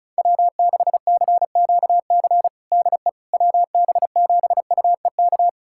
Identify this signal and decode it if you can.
Morse code, heard as a keyed tone, about 34 wpm, 700 Hz. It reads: W6CQC DE WB7UEK
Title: USA Callsign DE Callsign 35wpm